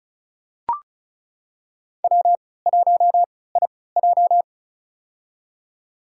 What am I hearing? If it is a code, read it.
Morse code, heard as a keyed tone, about 34 wpm, spaced as for 20 wpm, 700 Hz. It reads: W1IJ
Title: USA Call Signs - 4 Characters - 3x Character Spacing - 35wpm